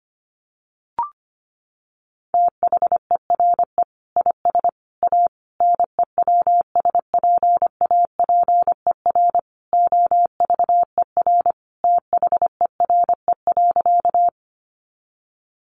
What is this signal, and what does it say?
Morse code, heard as a keyed tone, about 25 wpm, 690 Hz. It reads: THERE IS A NEWSPAPER OVER THERE.
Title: Sentences from Top 1100 Words 25wpm